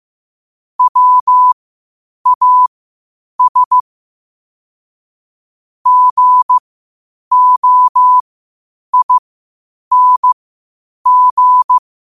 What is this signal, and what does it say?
Morse code, heard as a keyed tone, about 15 wpm, spaced as for 8 wpm, 1000 Hz. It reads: WAS GOING
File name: qotdfiles / 8wpm